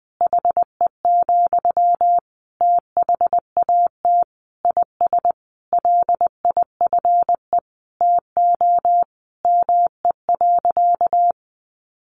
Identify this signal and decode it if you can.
Morse code, heard as a keyed tone, about 19 wpm, 700 Hz. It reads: HE, THAT IS LIFE TO ME.
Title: Sentences from Top 500 Words 20wpm